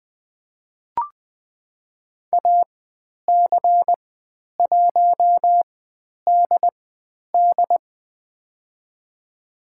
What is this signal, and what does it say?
Morse code, heard as a keyed tone, about 20 wpm, spaced as for 10 wpm, 700 Hz. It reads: AC1DD
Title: 095 Lesson 16g - Letter c - Call Signs 20wpm